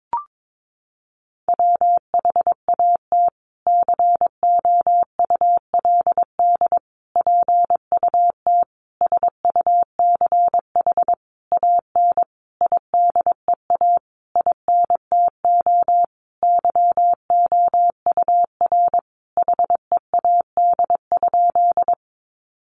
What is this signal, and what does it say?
Morse code, heard as a keyed tone, about 22 wpm, 700 Hz. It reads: WHAT COULD PUT SUCH AN IDEA INTO YOUR HEAD?
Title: Sentences from Top 600 Words 22wpm